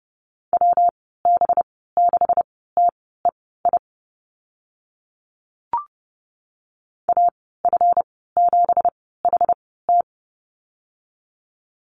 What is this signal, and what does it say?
Morse code, heard as a keyed tone, about 30 wpm, spaced as for 17 wpm, 700 Hz. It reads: WB6TEI AF7HT
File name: Lesson 37g - Letter x - Call Signs 30wpm